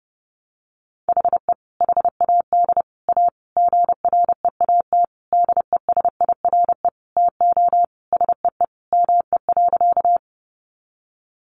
Morse code, heard as a keyed tone, about 30 wpm, 700 Hz. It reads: HE HAD A GREAT DESIRE TO SEE ME.
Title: Sentences from Top 1100 Words 30wpm